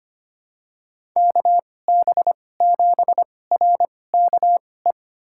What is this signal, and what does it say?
Morse code, heard as a keyed tone, about 25 wpm, spaced as for 18 wpm, 700 Hz. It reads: KB7RKE
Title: USA Call Signs - 6 Characters - 2x Character Spacing - 25wpm